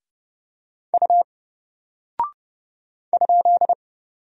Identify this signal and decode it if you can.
Morse code, heard as a keyed tone, about 29 wpm, 700 Hz. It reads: U ?
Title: Single Character (Letter:Number:Punctuation:Prosign) 30wpm